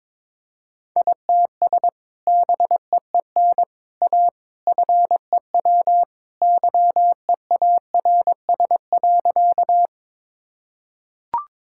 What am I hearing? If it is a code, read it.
Morse code, heard as a keyed tone, about 22 wpm, 700 Hz. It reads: ITS BEEN A FEW YEARS.
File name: practice / 22wpm / Sentences with 3-Word Phrases - 22wpm